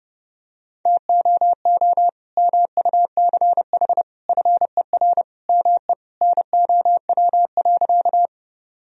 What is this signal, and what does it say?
Morse code, heard as a keyed tone, about 32 wpm, 700 Hz. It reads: TOO MUCH FER ME NOW.
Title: Sentences from Top 200 Words and Abbreviations 30wpm